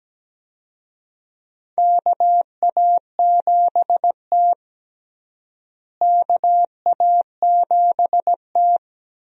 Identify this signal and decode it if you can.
Morse code, heard as a keyed tone, about 17 wpm, 700 Hz. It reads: KA7T KA7T
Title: USA Call Signs - 4 Characters - Repeated 2x - 17wpm